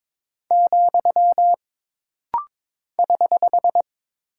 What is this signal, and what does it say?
Morse code, heard as a keyed tone, about 22 wpm, 700 Hz. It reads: , <HH>
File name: practice / 22wpm / Single Punctuation-Prosign - Rapid-Fire - 22wpm